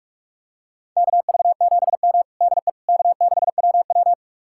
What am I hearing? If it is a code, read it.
Morse code, heard as a keyed tone, about 44 wpm, 700 Hz. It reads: KV7M DE K6WW
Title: USA Callsign DE Callsign - 4 Characters - 45wpm